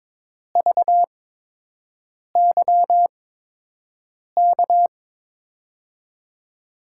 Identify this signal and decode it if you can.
Morse code, heard as a keyed tone, about 22 wpm, spaced as for 6 wpm, 700 Hz. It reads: VYK
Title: USA Call Signs - 8x Character Spacing - 22wpm